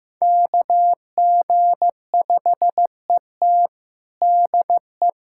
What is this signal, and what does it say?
Morse code, heard as a keyed tone, about 15 wpm, 700 Hz. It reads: KG5ET DE
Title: USA Callsign DE Callsign - 5 Characters - 15wpm